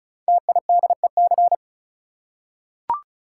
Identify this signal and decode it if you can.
Morse code, heard as a keyed tone, about 36 wpm, 690 Hz. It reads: TIDEC
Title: Code Groups - 5 Letters - 35wpm